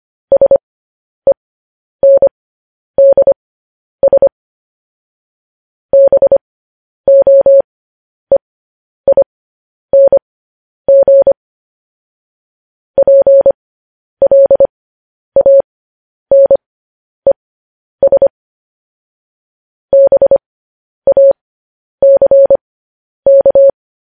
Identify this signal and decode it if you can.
Morse code, heard as a keyed tone, about 25 wpm, spaced as for 10 wpm, 550 Hz. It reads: SENDS BOEING PLANES BACK
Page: news headlines in morse code at 10 wpm